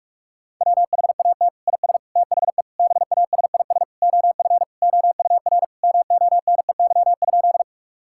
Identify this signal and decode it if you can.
Morse code, heard as a keyed tone, about 46 wpm, 700 Hz. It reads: WHAT IS THE BASIS OF OUR MONEY?
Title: Sentences from Top 1000 Words - Review Part 5 - 45wpm